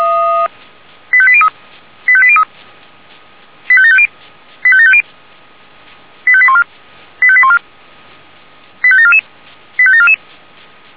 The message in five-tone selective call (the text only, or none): 85803 85803 85750 85750 85814 85814 85744 85744